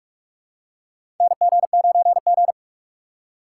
Morse code, heard as a keyed tone, about 41 wpm, 700 Hz. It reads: NG9G
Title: USA Call Signs - 4 Characters - 2x Character Spacing - 45wpm